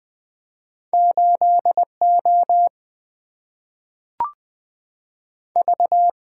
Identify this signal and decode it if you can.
Morse code, heard as a keyed tone, about 20 wpm, 700 Hz. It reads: 8O V
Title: Two Letter-Number Combinations - 20wpm